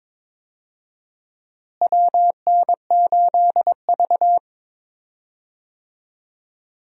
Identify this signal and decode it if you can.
Morse code, heard as a keyed tone, about 22 wpm, 700 Hz. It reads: WN8V
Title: USA Call Signs - 4 Characters - Repeated 2x - 22wpm